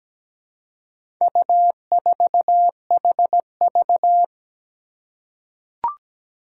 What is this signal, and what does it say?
Morse code, heard as a keyed tone, about 17 wpm, 700 Hz. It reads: U4HV
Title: Four Challenging Letter-Number Combinations 17wpm